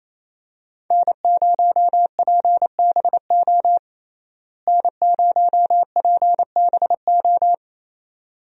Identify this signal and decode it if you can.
Morse code, heard as a keyed tone, about 28 wpm, 700 Hz. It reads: N0PBO N0PBO
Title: USA Call Signs - 5 Characters - Repeated 2x - 28wpm